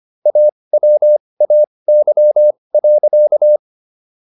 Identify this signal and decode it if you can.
Morse code, heard as a keyed tone, about 25 wpm, spaced as for 20 wpm, 600 Hz. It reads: AWAY.